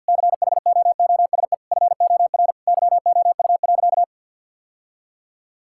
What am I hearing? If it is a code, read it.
Morse code, heard as a keyed tone, about 50 wpm, 700 Hz. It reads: CHOOSE FOR YOU.
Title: Sentences from Top 1000 Words - Review Part 6 - 50wpm